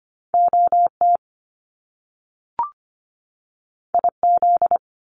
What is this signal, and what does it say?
Morse code, heard as a keyed tone, about 25 wpm, 700 Hz. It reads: OT IZ
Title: Top Two Letter Combinations in Words 25wpm